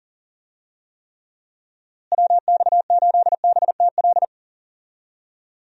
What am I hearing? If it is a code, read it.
Morse code, heard as a keyed tone, about 40 wpm, 700 Hz. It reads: WX8BTL
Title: USA Call Signs - 6 Characters - Repeated 3x - 40wpm